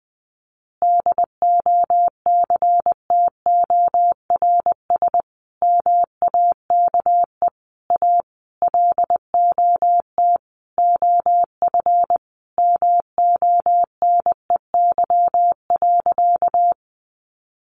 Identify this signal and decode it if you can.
Morse code, heard as a keyed tone, about 20 wpm, 700 Hz. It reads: DOCTORS MAKE A LOT OF MONEY.
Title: Easy Phrases 20wpm